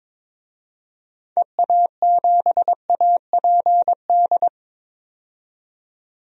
Morse code, heard as a keyed tone, about 22 wpm, 700 Hz. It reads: EA7APD
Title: DXCC-100 Call Signs - 6 Characters - 22wpm